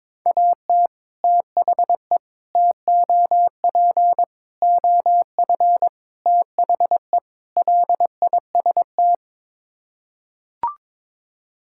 Morse code, heard as a keyed tone, about 22 wpm, 700 Hz. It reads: AT THE TOP OF THE LIST
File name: practice / 22wpm / Sets of 6 Words - 22wpm